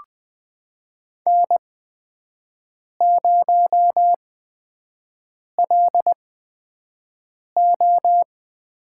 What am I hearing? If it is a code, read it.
Morse code, heard as a keyed tone, about 20 wpm, spaced as for 5 wpm, 700 Hz. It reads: N0LO
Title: USA Call Signs - 5 Characters - 8x Character Spacing - 20wpm